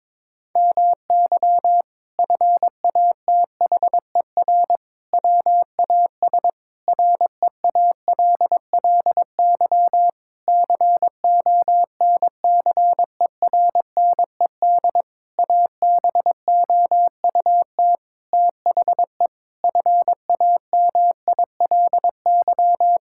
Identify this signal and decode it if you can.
Morse code, heard as a keyed tone, about 23 wpm, 700 Hz. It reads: MY FATHER WAS REALLY CONCERNED ABOUT THE FAMILY
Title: Sentences from Top 1000 Words - Review Part 6 - 22wpm